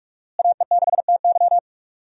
Morse code, heard as a keyed tone, about 46 wpm, 700 Hz. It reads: AE6TY
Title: USA Call Signs - 5 Characters - Repeated 2x - 45wpm